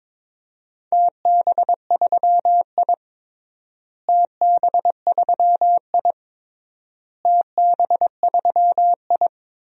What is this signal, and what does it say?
Morse code, heard as a keyed tone, about 22 wpm, 700 Hz. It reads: TB3I TB3I TB3I
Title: DXCC-100 Call Signs - 4 Characters - Repeated 3x - 22wpm